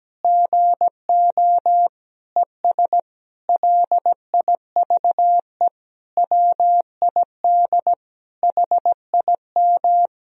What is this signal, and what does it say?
Morse code, heard as a keyed tone, about 17 wpm, 700 Hz. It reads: GO ES LIVE WID HIM
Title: Sentences from Top 500 Words and Abbreviations 17wpm